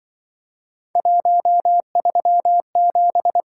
Morse code, heard as a keyed tone, about 24 wpm, 700 Hz. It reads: 137
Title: Speed-Racing - Three Letter-Number Combinations - 20wpm